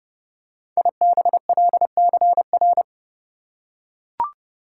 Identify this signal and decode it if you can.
Morse code, heard as a keyed tone, about 30 wpm, 700 Hz. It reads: IBLCR
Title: Code Groups - 5 Letter:Numbers - 30wpm